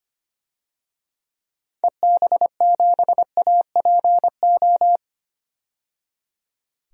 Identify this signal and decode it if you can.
Morse code, heard as a keyed tone, about 25 wpm, 700 Hz. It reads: EB7APO
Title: DXCC-100 Call Signs - 6 Characters - 25wpm